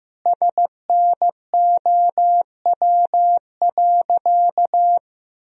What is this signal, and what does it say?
Morse code, heard as a keyed tone, about 15 wpm, 700 Hz. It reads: SNOW.
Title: Sentences from Top 1000 Words - Review Part 3 - 15wpm